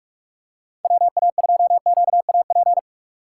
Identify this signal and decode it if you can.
Morse code, heard as a keyed tone, about 46 wpm, 700 Hz. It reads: WA2QAP
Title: USA Call Signs - 6 Characters - 6x Character Spacing - 45wpm